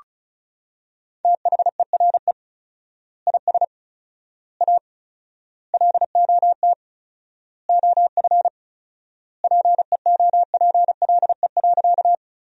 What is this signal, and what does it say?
Morse code, heard as a keyed tone, about 34 wpm, 700 Hz. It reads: THERE IS A LOT OF PEOPLE.
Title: 4x Word-Spacing - Sentences from Top 500 Words - 35wpm